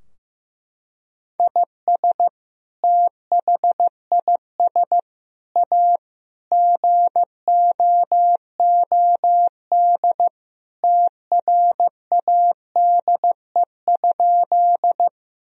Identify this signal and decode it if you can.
Morse code, heard as a keyed tone, about 15 wpm, 700 Hz. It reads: IS THIS A GOOD TRADE?
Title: Sentences from Top 1000 Words - Review Part 4 - 15wpm